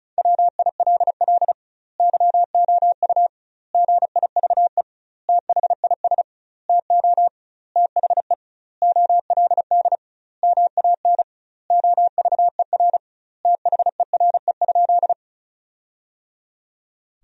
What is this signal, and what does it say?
Morse code, heard as a keyed tone, about 36 wpm, 700 Hz. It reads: WILL YOU GIVE THIS TO THE OLD MAN OVER THERE?
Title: 2x Word-Spacing - Sentences from Top 200 Words - 35wpm